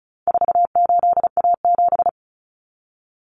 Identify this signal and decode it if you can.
Morse code, heard as a keyed tone, about 34 wpm, 700 Hz. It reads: 48A7
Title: Common Cut Numbers - 4-Digit 35wpm